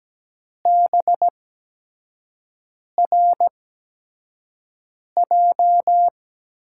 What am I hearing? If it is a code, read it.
Morse code, heard as a keyed tone, about 17 wpm, spaced as for 5 wpm, 700 Hz. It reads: BRJ
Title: USA Call Signs - 5 Characters - 8x Character Spacing - 17wpm